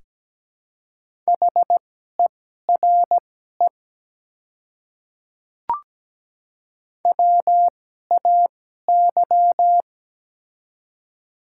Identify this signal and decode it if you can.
Morse code, heard as a keyed tone, about 17 wpm, spaced as for 12 wpm, 700 Hz. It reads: HERE WAY
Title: 2x Character Spacing - Top 200 Words - 17wpm